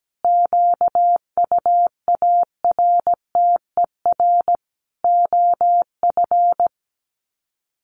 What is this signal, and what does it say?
Morse code, heard as a keyed tone, about 17 wpm, 700 Hz. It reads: QUARTER OF
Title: Sets of 5 Words Encore2 17wpm